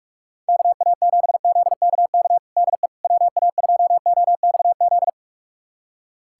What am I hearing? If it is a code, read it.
Morse code, heard as a keyed tone, about 46 wpm, 700 Hz. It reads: KA7ZKK DE WA2OXZ
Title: USA Callsign DE Callsign - 6 Characters - 45wpm